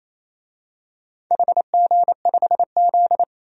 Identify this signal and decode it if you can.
Morse code, heard as a keyed tone, about 28 wpm, 700 Hz. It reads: HG5Z
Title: DXCC-100 Call Signs - 4 Characters - Repeated 2x - 28wpm